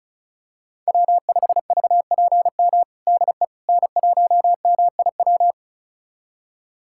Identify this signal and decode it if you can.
Morse code, heard as a keyed tone, about 34 wpm, 700 Hz. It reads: W5VPM DE N1MIW
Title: USA Callsign DE Callsign - 5 Characters - 35wpm